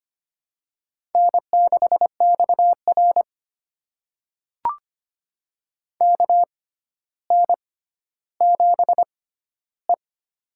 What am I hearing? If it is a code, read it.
Morse code, heard as a keyed tone, about 25 wpm, spaced as for 13 wpm, 700 Hz. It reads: N6XR KN7E
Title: USA Call Signs - 4 Characters - 6x Character Spacing - 25wpm